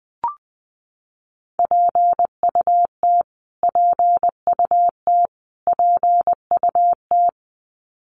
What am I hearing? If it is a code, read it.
Morse code, heard as a keyed tone, about 19 wpm, 700 Hz. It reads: PUT PUT PUT
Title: Top 500 Words Repeated 20wpm